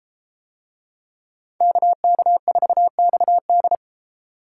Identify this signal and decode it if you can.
Morse code, heard as a keyed tone, about 31 wpm, 700 Hz. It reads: KK4XD